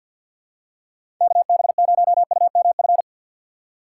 Morse code, heard as a keyed tone, about 47 wpm, 700 Hz. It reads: KB0UMF
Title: USA Call Signs - 6 Characters - 6x Character Spacing - 50wpm